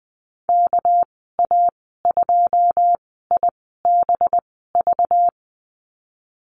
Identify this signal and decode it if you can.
Morse code, heard as a keyed tone, about 19 wpm, spaced as for 14 wpm, 700 Hz. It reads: KA2IBV
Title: 245 Lesson 39g - Letter z - Call Signs 20wpm